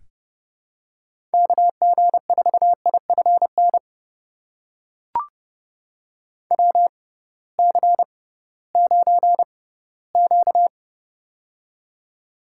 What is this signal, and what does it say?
Morse code, heard as a keyed tone, about 30 wpm, spaced as for 18 wpm, 700 Hz. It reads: KG4IFN WC9Q
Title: USA Call Signs - 6x Character Spacing - 30wpm